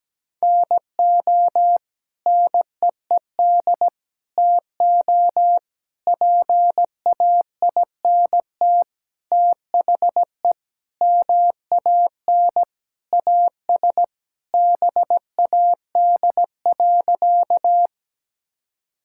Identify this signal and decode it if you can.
Morse code, heard as a keyed tone, about 17 wpm, 700 Hz. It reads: NO NEED TO PAINT THE MAN AS BAD.
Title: Sentences from Top 1000 Words - Review Part 6 - 17wpm